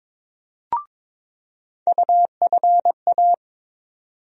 Morse code, heard as a keyed tone, about 22 wpm, 700 Hz. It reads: UFA
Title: USA Call Sign Postfixes - 3 Letters 22wpm